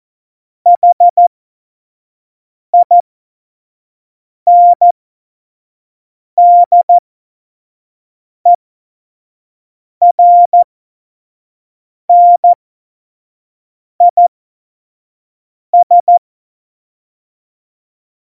Morse code, heard as a keyed tone, about 14 wpm, spaced as for 5 wpm, 700 Hz. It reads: HINDERNIS